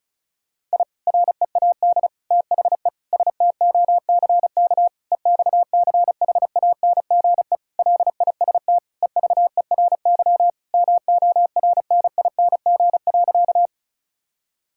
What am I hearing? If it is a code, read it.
Morse code, heard as a keyed tone, about 36 wpm, 700 Hz. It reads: I READ THE STOCK EXCHANGE LIST EVERY MORNING.
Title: Sentences from Top 1000 Words - Review Part 6 - 35wpm